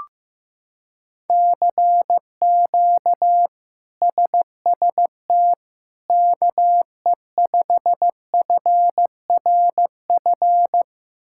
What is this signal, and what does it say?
Morse code, heard as a keyed tone, about 15 wpm, 700 Hz. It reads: CQ SST KE5FRF
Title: K1USN SST QSO Practice - 15wpm